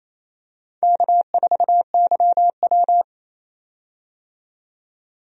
Morse code, heard as a keyed tone, about 28 wpm, 700 Hz. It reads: K4YW
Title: USA Call Signs - 4 Characters - 28wpm